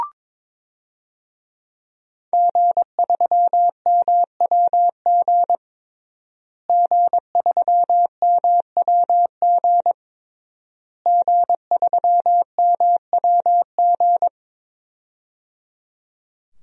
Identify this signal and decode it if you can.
Morse code, heard as a keyed tone, about 22 wpm, 700 Hz. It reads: G3MWG G3MWG G3MWG
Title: DXCC-100 Call Signs - 5 Characters - Repeated 3x - 22wpm